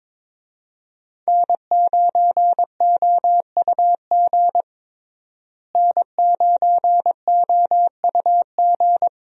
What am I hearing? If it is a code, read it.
Morse code, heard as a keyed tone, about 21 wpm, 700 Hz. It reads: N9OUG N9OUG
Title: USA Call Signs - 5 Characters - Repeated 2x - 22wpm